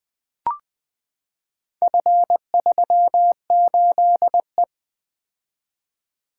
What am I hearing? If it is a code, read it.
Morse code, heard as a keyed tone, about 20 wpm, 700 Hz. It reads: F38E
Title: Four Letter-Number Combinations - 20wpm